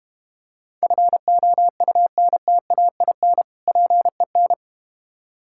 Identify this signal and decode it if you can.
Morse code, heard as a keyed tone, about 32 wpm, 700 Hz. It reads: FOUNTAIN PEN